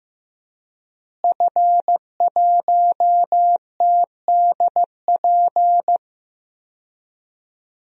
Morse code, heard as a keyed tone, about 16 wpm, 700 Hz. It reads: F1TDP
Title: DXCC-100 Call Signs - 5 Characters - Repeated 2x - 15wpm